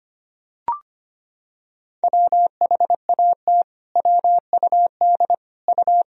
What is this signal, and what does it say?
Morse code, heard as a keyed tone, about 25 wpm, 700 Hz. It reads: WHAT WUD U
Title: Sentences from Top 300 Words and Abbreviations 25wpm